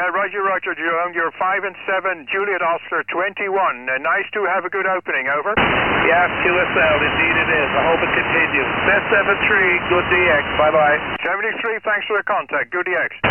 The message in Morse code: none